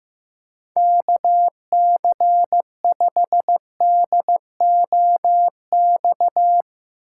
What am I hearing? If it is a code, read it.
Morse code, heard as a keyed tone, about 15 wpm, 700 Hz. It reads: KC5DOX
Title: USA Call Signs - 6 Characters - 4x Character Spacing - 15wpm